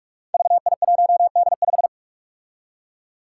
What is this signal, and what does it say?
Morse code, heard as a keyed tone, about 46 wpm, 700 Hz. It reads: VI1D5
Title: Five Challenging Letter-Number Combinations 45wpm